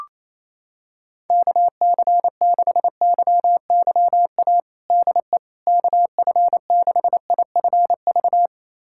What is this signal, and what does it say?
Morse code, heard as a keyed tone, about 27 wpm, 700 Hz. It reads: KC6YYA DE KF6IFV
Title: USA Callsign DE Callsign - 6 Characters - 28wpm